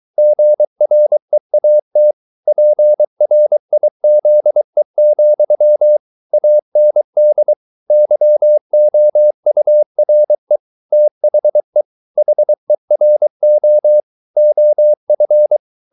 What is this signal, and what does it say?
Morse code, heard as a keyed tone, about 23 wpm, 600 Hz. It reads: GREAT PRIZE, AND YOURE THE HERO OF